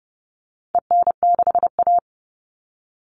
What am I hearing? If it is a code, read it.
Morse code, heard as a keyed tone, about 30 wpm, 700 Hz. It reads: EN6A
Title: All Cut Numbers - 4-Digit 30wpm